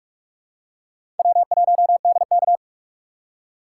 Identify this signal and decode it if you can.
Morse code, heard as a keyed tone, about 44 wpm, 700 Hz. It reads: W1DK
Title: USA Call Signs - 4 Characters - 6x Character Spacing - 45wpm